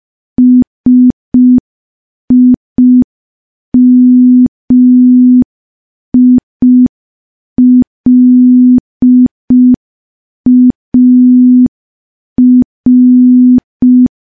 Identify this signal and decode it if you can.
Morse code, heard as a keyed tone, about 5 wpm, 250 Hz. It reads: SIMILAR
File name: morse / morse-training-05WPM